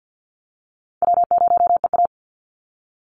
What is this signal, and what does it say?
Morse code, heard as a keyed tone, about 50 wpm, 700 Hz. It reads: W0EA
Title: Call Signs Repeated 3x - 50wpm